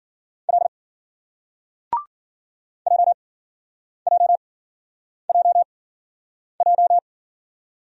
Morse code, heard as a keyed tone, about 55 wpm, spaced as for 9 wpm, 700 Hz. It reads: L JJJJ